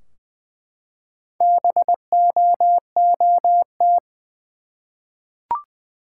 Morse code, heard as a keyed tone, about 20 wpm, 700 Hz. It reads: BOOT
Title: Four Letter Words - 20wpm